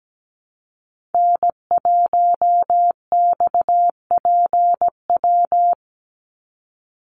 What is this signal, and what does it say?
Morse code, heard as a keyed tone, about 17 wpm, 700 Hz. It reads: N1XPW
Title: USA Call Signs 17wpm